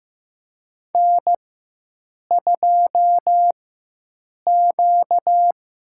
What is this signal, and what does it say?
Morse code, heard as a keyed tone, about 15 wpm, 700 Hz. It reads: N 2 Q
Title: USA Call Signs - 4 Characters - 4x Character Spacing - 15wpm